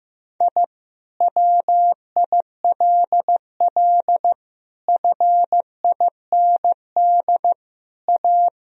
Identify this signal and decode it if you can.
Morse code, heard as a keyed tone, about 16 wpm, 700 Hz. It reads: I WILL FIND A
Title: Sentences from Top 1000 Words - Review Part 1 - 15wpm